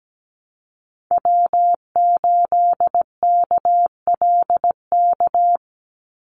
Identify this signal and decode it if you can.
Morse code, heard as a keyed tone, about 17 wpm, 700 Hz. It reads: W8KLK
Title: USA Call Signs 17wpm